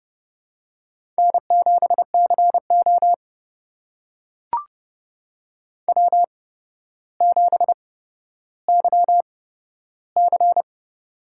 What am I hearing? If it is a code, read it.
Morse code, heard as a keyed tone, about 30 wpm, spaced as for 13 wpm, 700 Hz. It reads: N7CO W7YC
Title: USA Call Signs - 4 Characters - 8x Character Spacing - 30wpm